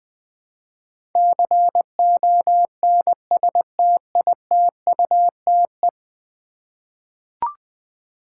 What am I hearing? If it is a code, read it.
Morse code, heard as a keyed tone, about 20 wpm, 700 Hz. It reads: CONSTITUTE
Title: 10 Letter Words - First 3 Letters Given - 20wpm